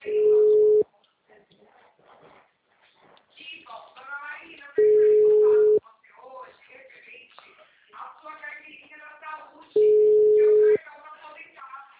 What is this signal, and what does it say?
Ringback tone (Europe, 425 Hz), with speech.
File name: out- -2033- - - .51100